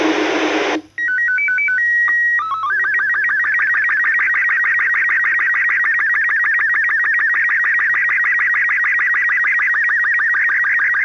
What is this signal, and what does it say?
SSTV (slow-scan television), Robot 36.
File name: iss 2 april 13 2026-1